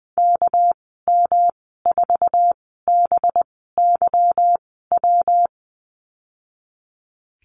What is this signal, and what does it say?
Morse code, heard as a keyed tone, about 20 wpm, spaced as for 15 wpm, 700 Hz. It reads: KM4BYW
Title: 200 Lesson 32g - Letter k - Call Signs 20wpm